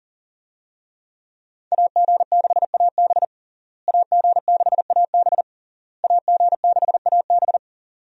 Morse code, heard as a keyed tone, about 40 wpm, 700 Hz. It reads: AG6AB AG6AB AG6AB
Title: USA Call Signs - 5 Characters - Repeated 3x - 40wpm